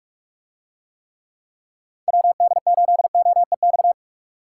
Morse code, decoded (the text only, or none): WD8OEX